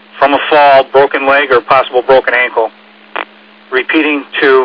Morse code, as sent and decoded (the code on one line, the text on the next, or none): none